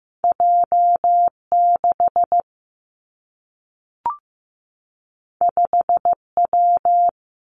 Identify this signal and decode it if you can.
Morse code, heard as a keyed tone, about 15 wpm, 700 Hz. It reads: J 6 5 W
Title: Two Challenging Letter-Number Combinations 15wpm